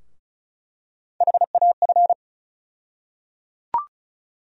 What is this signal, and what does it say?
Morse code, heard as a keyed tone, about 34 wpm, 700 Hz. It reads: HAF
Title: USA Call Sign Postfixes - 3 Letters 35wpm